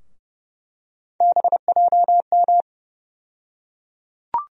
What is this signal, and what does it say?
Morse code, heard as a keyed tone, about 31 wpm, 700 Hz. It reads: BJM
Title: USA Call Sign Postfixes - 3 Letters 30wpm